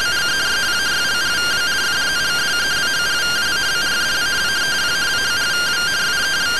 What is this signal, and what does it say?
Two-tone teleprinter signal, about 170 Hz shift.